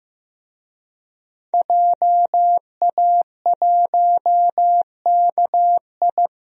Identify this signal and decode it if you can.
Morse code, heard as a keyed tone, about 15 wpm, 700 Hz. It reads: JA1KI
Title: DXCC-100 Call Signs - 5 Characters - 15wpm